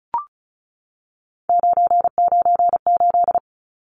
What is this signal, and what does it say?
Morse code, heard as a keyed tone, about 34 wpm, 700 Hz. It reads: 998
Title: Numbers - 3-Digit 35wpm